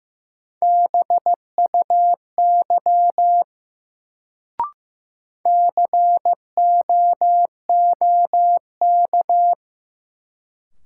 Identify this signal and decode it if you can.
Morse code, heard as a keyed tone, about 15 wpm, 700 Hz. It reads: BUY COOK
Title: Sentences from Top 700 Words - New Word Drill - 15wpm